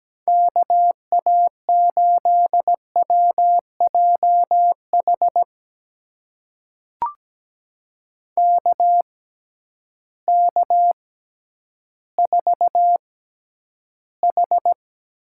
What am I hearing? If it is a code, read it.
Morse code, heard as a keyed tone, about 17 wpm, spaced as for 10 wpm, 700 Hz. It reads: KA8WJH KK4H
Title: USA Call Signs - 6 Characters - 6x Character Spacing - 17wpm